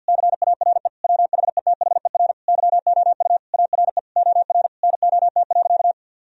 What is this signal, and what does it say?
Morse code, heard as a keyed tone, about 53 wpm, 700 Hz. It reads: CARE WHETHER YOU ARE OR NOT.
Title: Sentences from Top 1000 Words - Review Part 3 - 50wpm